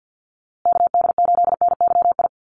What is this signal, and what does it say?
Morse code, heard as a keyed tone, about 48 wpm, 700 Hz. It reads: KD8NYI
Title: USA Call Signs Repeated 3x 50wpm